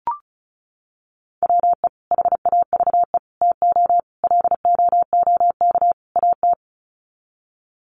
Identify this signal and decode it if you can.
Morse code, heard as a keyed tone, about 34 wpm, 700 Hz. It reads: WE HAVE TO LOOK AT
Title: Sets of 5 Words Encore2 35wpm